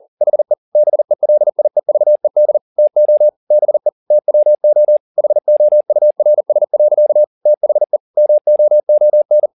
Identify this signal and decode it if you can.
Morse code, heard as a keyed tone, about 40 wpm, 600 Hz. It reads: HE BELIEVED TO BE TWO HOURS. THE MOON
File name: the old man and the sea 2 40wpm